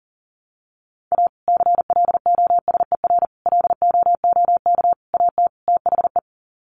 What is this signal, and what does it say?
Morse code, heard as a keyed tone, about 40 wpm, 700 Hz. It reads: A CLOSER LOOK AT THE
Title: Sets of 5 Words Encore 40wpm